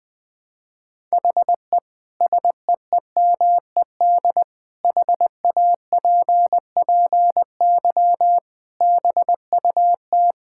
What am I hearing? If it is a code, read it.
Morse code, heard as a keyed tone, about 20 wpm, 700 Hz. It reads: HE SEEMED HAPPY BUT
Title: Sentences with 8-Word Phrases - 20wpm